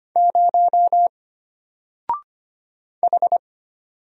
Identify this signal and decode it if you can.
Morse code, heard as a keyed tone, about 25 wpm, 700 Hz. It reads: 0 H
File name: Single-Letter-Number-25wpm